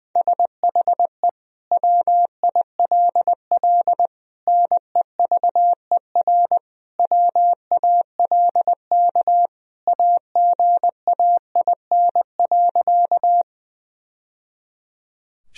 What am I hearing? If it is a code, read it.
Morse code, heard as a keyed tone, about 20 wpm, 700 Hz. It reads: SHE WILL NEVER WALK AGAIN.
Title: Sentences from Top 1000 Words - Review Part 1 - 20wpm